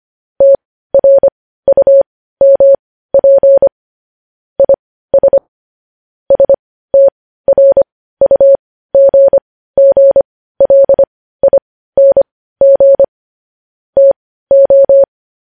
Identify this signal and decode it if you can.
Morse code, heard as a keyed tone, about 25 wpm, spaced as for 15 wpm, 550 Hz. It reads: TRUMP IS STRUGGLING TO
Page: news headlines in morse code at 15 wpm